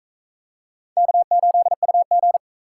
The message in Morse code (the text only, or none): K8UG